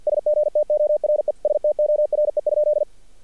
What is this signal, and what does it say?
Morse code, heard as a keyed tone, about 47 wpm, 600 Hz. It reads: SGTORE STORE?
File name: archived / CRSnet